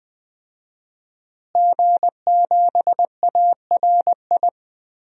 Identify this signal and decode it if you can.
Morse code, heard as a keyed tone, about 20 wpm, 700 Hz. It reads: G7ARI
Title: DXCC-100 Call Signs - 5 Characters - Repeated 3x - 20wpm